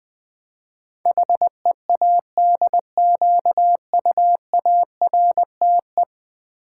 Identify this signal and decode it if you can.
Morse code, heard as a keyed tone, about 20 wpm, 700 Hz. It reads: HEADQUARTE
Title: Build A Word - 5 Letters At A Time - 12-Letter Words - 20wpm